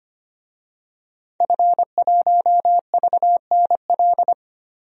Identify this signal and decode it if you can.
Morse code, heard as a keyed tone, about 25 wpm, 700 Hz. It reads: F1VNL
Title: DXCC-100 Call Signs - 5 Characters - 25wpm